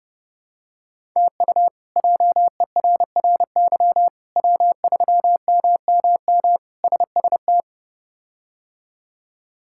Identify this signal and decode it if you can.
Morse code, heard as a keyed tone, about 30 wpm, 700 Hz. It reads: TU JERRY W3MMM SST
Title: K1USN SST QSO Practice - 30wpm